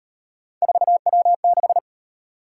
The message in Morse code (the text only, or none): 4W6